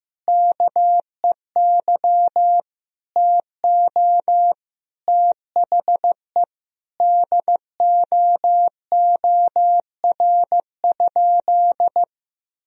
Morse code, heard as a keyed tone, about 15 wpm, 700 Hz. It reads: KEY TO THE DOOR?
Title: Sentences from Top 1000 Words - Review Part 4 - 15wpm